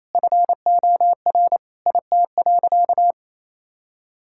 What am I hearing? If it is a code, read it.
Morse code, heard as a keyed tone, about 28 wpm, 700 Hz. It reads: FOR IT.
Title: Sentences from Top 1000 Words - Review Part 6 - 28wpm